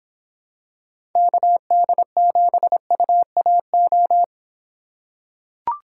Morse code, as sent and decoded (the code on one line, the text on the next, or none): -.- -.. --... ..- .- ---
KD7UAO